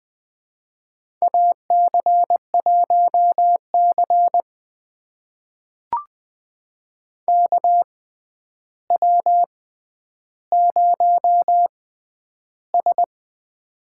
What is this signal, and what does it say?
Morse code, heard as a keyed tone, about 20 wpm, spaced as for 10 wpm, 700 Hz. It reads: AC1C KW0S
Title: USA Call Signs - 4 Characters - 6x Character Spacing - 20wpm